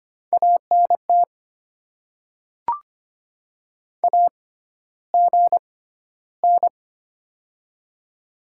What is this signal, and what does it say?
Morse code, heard as a keyed tone, about 25 wpm, spaced as for 13 wpm, 700 Hz. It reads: ANT AGN
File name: Top 20 QSO Elements - 6x Character Spacing - 25wpm